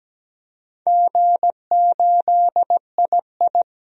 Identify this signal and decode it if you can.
Morse code, heard as a keyed tone, about 17 wpm, 700 Hz. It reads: G8II
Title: DXCC-100 Call Signs - 4 Characters - 17wpm